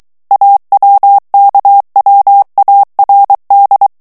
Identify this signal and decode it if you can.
Morse code, heard as a keyed tone, about 23 wpm, 800 Hz. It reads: AWKWARD
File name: morse mp3 / ANGLAIS texte clair 2 vitesse 22 mots minute